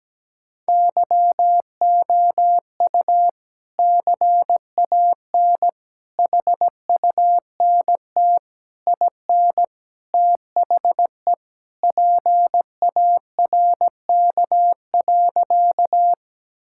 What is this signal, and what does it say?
Morse code, heard as a keyed tone, about 17 wpm, 700 Hz. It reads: YOU CAN HUNT IN THE PARK.
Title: Sentences from Top 1600 Words 17wpm